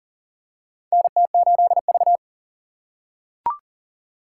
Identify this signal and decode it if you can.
Morse code, heard as a keyed tone, about 39 wpm, 700 Hz. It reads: NT8V
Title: USA Call Signs - 3x Character Spacing - 40wpm